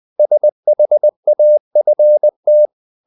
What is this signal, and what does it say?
Morse code, heard as a keyed tone, about 20 wpm, 600 Hz. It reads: SHAFT
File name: the old man and the sea part 2 20wpm